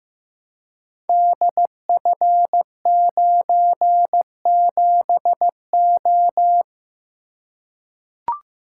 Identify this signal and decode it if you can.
Morse code, heard as a keyed tone, about 15 wpm, 700 Hz. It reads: DF97O
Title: Code Groups - 5 Letter:Numbers - 15wpm